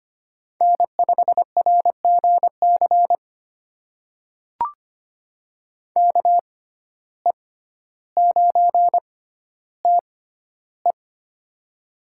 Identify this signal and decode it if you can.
Morse code, heard as a keyed tone, about 25 wpm, spaced as for 13 wpm, 700 Hz. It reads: N5RGC KE9TE
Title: USA Call Signs - 5 Characters - 6x Character Spacing - 25wpm